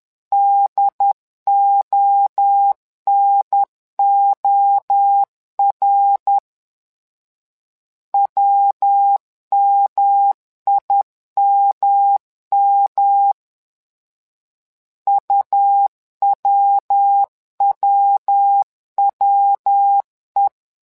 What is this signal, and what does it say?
Morse code, heard as a keyed tone, about 10 wpm, 800 Hz. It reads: DONOR WMIMM UWWWE
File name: LEZIONI / 3 MWRO / 3 MWRO 10wpm 2148